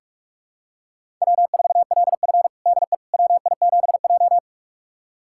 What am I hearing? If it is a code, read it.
Morse code, heard as a keyed tone, about 44 wpm, 690 Hz. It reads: W4LF DE WI7J
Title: USA Callsign DE Callsign - 4 Characters - 45wpm